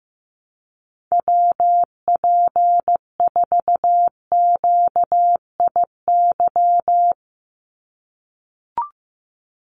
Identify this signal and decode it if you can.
Morse code, heard as a keyed tone, about 15 wpm, 700 Hz. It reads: WP4QIY
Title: USA Call Signs 15wpm